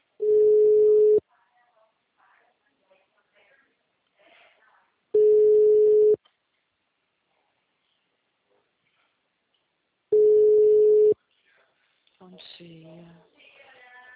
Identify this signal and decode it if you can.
Ringback tone (Europe, 425 Hz), with speech.